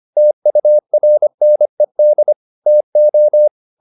Morse code, heard as a keyed tone, about 25 wpm, 600 Hz. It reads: TURNED TO
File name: the old man and the sea part 1 25wpm